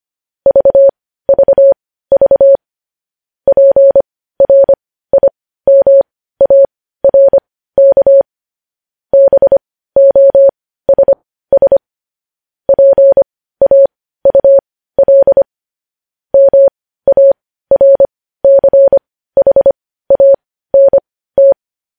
Morse code, heard as a keyed tone, about 25 wpm, spaced as for 15 wpm, 550 Hz. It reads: VVV PRIMARK BOSS PAUL MARCHANT
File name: podcast / News headlines-1x-25x15